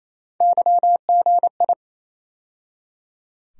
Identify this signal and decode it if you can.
Morse code, heard as a keyed tone, about 28 wpm, 700 Hz. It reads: YGI
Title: USA Call Sign Postfixes - 3 Letters 28wpm